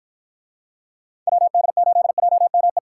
Morse code, heard as a keyed tone, about 52 wpm, 700 Hz. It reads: WD8JME